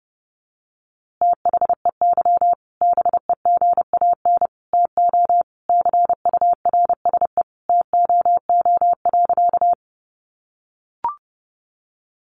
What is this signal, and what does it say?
Morse code, heard as a keyed tone, about 30 wpm, 700 Hz. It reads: THEY BEGAN TO CURSE TOO.
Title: A Sound Of Thunder 30wpm